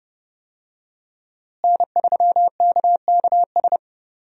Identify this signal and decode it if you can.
Morse code, heard as a keyed tone, about 30 wpm, 700 Hz. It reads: N3KKS
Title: USA Call Signs - 5 Characters - Repeated 3x - 30wpm